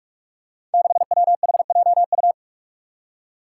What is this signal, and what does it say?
Morse code, heard as a keyed tone, about 44 wpm, 700 Hz. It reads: 6WHJU
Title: Five Challenging Letter-Number Combinations 45wpm